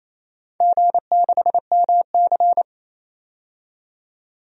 Morse code, heard as a keyed tone, about 27 wpm, 700 Hz. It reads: G6MC
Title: DXCC-100 Call Signs - 4 Characters - 28wpm